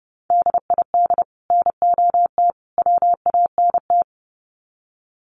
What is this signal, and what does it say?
Morse code, heard as a keyed tone, about 30 wpm, 700 Hz. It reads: DID NOT WANT
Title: Sets of 3 Words 30wpm